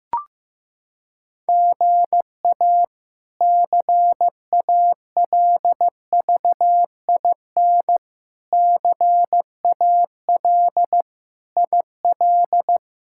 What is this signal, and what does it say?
Morse code, heard as a keyed tone, about 15 wpm, 700 Hz. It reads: GA CALVIN CAL IL
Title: K1USN SST QSO Practice - 15wpm